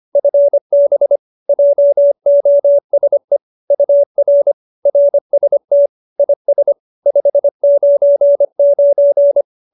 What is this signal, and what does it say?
Morse code, heard as a keyed tone, about 25 wpm, 570 Hz. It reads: FB JOSE UR RST IS 599